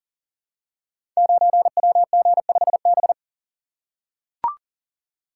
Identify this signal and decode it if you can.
Morse code, heard as a keyed tone, about 39 wpm, 690 Hz. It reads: 9WG5B
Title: Code Groups - 5 Letter:Numbers - 40wpm